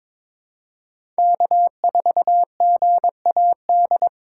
Morse code, heard as a keyed tone, about 22 wpm, 700 Hz. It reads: K4GAD
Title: USA Call Signs - 5 Characters - 6x Character Spacing - 22wpm